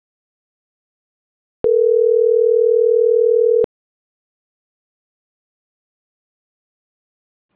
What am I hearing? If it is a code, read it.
Ringback tone (North America), 440 + 480 Hz.